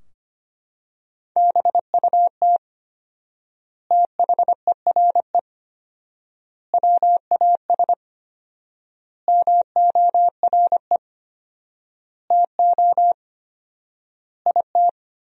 Morse code, heard as a keyed tone, about 25 wpm, 700 Hz. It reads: BUT THERE WAS MORE TO IT
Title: 4x Word-Spacing - Sentences from Top 100 Words - 25wpm